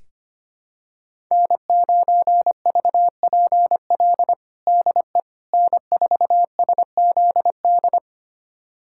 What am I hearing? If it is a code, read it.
Morse code, heard as a keyed tone, about 25 wpm, 700 Hz. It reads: N9VPL DE N4SZD
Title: USA Callsign DE Callsign - 5 Characters - 25wpm